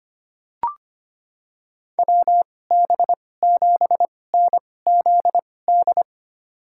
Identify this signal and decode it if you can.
Morse code, heard as a keyed tone, about 25 wpm, spaced as for 18 wpm, 700 Hz. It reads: WB7NZD
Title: USA Call Signs - 6 Characters - 2x Character Spacing - 25wpm